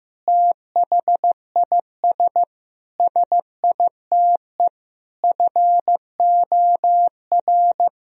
Morse code, heard as a keyed tone, about 15 wpm, 700 Hz. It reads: THIS SITE FOR
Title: Sentences from Top 1000 Words - Review Part 1 - 15wpm